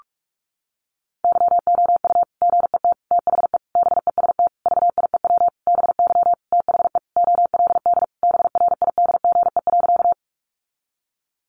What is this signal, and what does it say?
Morse code, heard as a keyed tone, about 44 wpm, 700 Hz. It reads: YOU GET THE BEST VIEW BY THE OLD BRIDGE.
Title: Sentences from Top 1200 Words 45wpm